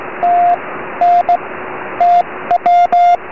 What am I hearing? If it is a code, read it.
Morse code, heard as a keyed tone, about 17 wpm, 700 Hz, with radio noise. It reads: T N T W
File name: 4m